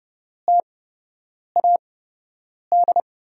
Morse code, heard as a keyed tone, about 30 wpm, spaced as for 8 wpm, 700 Hz. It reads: TAD
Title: USA Call Signs - 6 Characters - 8x Character Spacing - 30wpm